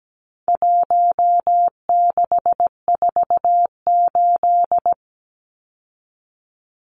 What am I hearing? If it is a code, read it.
Morse code, heard as a keyed tone, about 17 wpm, 700 Hz. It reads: 1648
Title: Numbers - 4-Digit 17wpm